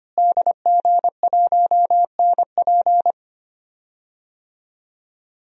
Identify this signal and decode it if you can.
Morse code, heard as a keyed tone, about 25 wpm, 700 Hz. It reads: DG1NP
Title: DXCC-100 Call Signs - 5 Characters - Repeated 2x - 25wpm